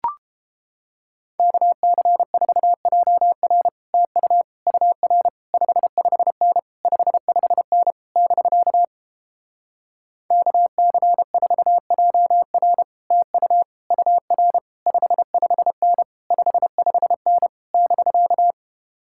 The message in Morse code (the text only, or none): KC4JR TU UR 55N 55N <BK> KC4JR TU UR 55N 55N <BK>